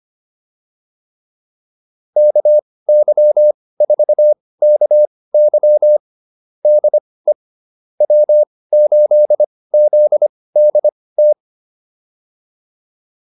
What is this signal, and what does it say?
Morse code, heard as a keyed tone, about 25 wpm, spaced as for 18 wpm, 600 Hz. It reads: KY4KY DE W8ZDT